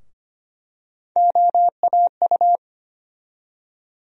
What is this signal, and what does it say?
Morse code, heard as a keyed tone, about 25 wpm, 700 Hz. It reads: OAU
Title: USA Call Sign Postfixes - 3 Letters 25wpm